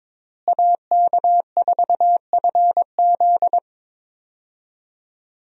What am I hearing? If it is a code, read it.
Morse code, heard as a keyed tone, about 22 wpm, 700 Hz. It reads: AK4FZ
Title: USA Call Signs - 5 Characters - Repeated 2x - 22wpm